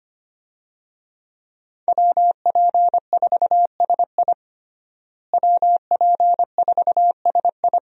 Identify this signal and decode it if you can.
Morse code, heard as a keyed tone, about 25 wpm, 700 Hz. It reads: WP4SI WP4SI
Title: USA Call Signs - 5 Characters - Repeated 3x - 25wpm